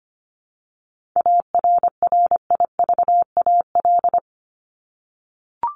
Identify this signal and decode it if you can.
Morse code, heard as a keyed tone, about 25 wpm, 700 Hz. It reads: ARRIVAL
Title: Top 1751-2000 Words 25wpm